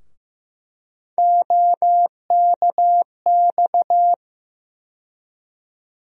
Morse code, heard as a keyed tone, about 15 wpm, 700 Hz. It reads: OKX
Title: USA Call Sign Postfixes - 3 Letters 15wpm